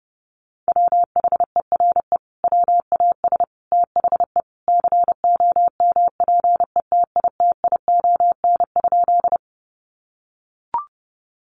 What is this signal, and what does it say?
Morse code, heard as a keyed tone, about 30 wpm, 700 Hz. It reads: WHERE WAS THE COMPETITION?
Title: Sentences from Top 1000 Words 30wpm